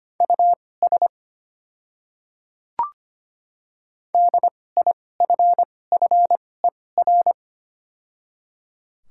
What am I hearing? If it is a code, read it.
Morse code, heard as a keyed tone, about 25 wpm, spaced as for 18 wpm, 700 Hz. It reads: US DIFFER
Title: 2x Character Spacing - Top 200 Words - 25wpm